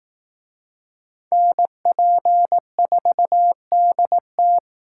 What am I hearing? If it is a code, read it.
Morse code, heard as a keyed tone, about 19 wpm, 700 Hz. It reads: NP4DT